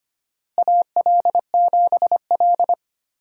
Morse code, heard as a keyed tone, about 25 wpm, 700 Hz. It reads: AL7L
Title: USA Call Signs - 4 Characters - Repeated 3x - 25wpm